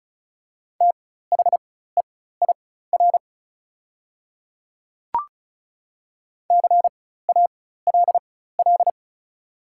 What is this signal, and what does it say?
Morse code, heard as a keyed tone, about 35 wpm, spaced as for 16 wpm, 700 Hz. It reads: THEIR CALL
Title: 4x Character Spacing - Top 300 Words - 35wpm